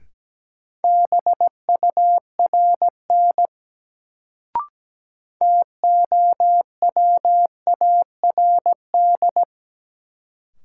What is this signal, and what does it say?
Morse code, heard as a keyed tone, about 17 wpm, 700 Hz. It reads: BURN TOWARD
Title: Sentences from Top 1300 Words - New Word Drill Alt - 17wpm